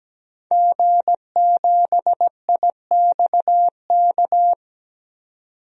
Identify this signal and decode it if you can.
Morse code, heard as a keyed tone, about 17 wpm, 700 Hz. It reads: G7IXK
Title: DXCC-100 Call Signs - 5 Characters - 17wpm